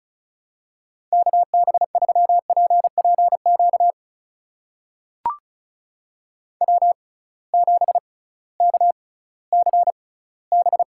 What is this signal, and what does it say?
Morse code, heard as a keyed tone, about 36 wpm, spaced as for 19 wpm, 700 Hz. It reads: KB3PPQ W7KCB
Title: USA Call Signs - 6x Character Spacing - 35wpm